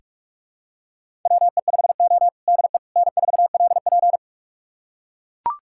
Morse code, heard as a keyed tone, about 41 wpm, 700 Hz. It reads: WE5O DE N4LP
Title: USA Callsign DE Callsign - 4 Characters - 45wpm